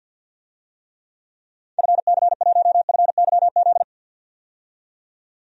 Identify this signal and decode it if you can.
Morse code, heard as a keyed tone, about 51 wpm, 700 Hz. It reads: FC1FYZ
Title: DXCC-100 Call Signs - 6 Characters - 50wpm